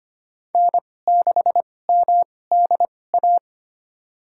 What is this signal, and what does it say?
Morse code, heard as a keyed tone, about 25 wpm, spaced as for 18 wpm, 700 Hz. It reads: N6MDA
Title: USA Call Signs - 5 Characters - 2x Character Spacing - 25wpm